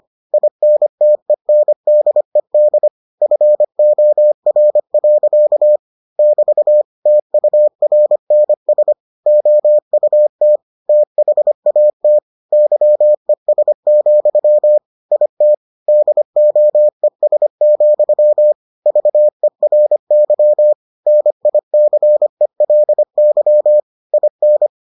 Morse code, heard as a keyed tone, about 25 wpm, 600 Hz. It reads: INTENDED FOR. <BT> TURNS OUT THAT YES, IT DOES, VERY NICELY IN